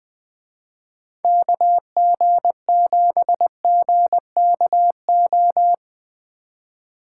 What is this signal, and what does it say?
Morse code, heard as a keyed tone, about 20 wpm, 700 Hz. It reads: KG7GKO
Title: Speed Racing - USA Call Signs 17wpm